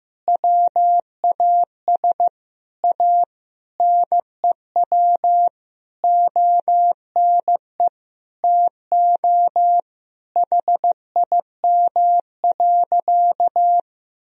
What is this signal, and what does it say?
Morse code, heard as a keyed tone, about 15 wpm, 700 Hz. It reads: WAS A NEW ONE TO HIM.
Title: Sentences from Top 1000 Words - Review Part 4 - 15wpm